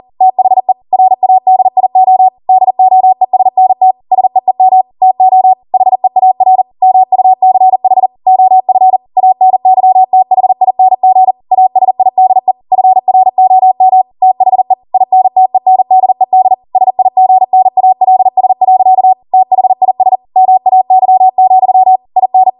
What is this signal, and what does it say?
Morse code, heard as a keyed tone, about 40 wpm, 750 Hz. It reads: THE RADIO DOESNT SEEM TO HEAR MUCH OF ANYTHING ASIDE FROM THE INTENDED SIGNALS. THIS MAY, IN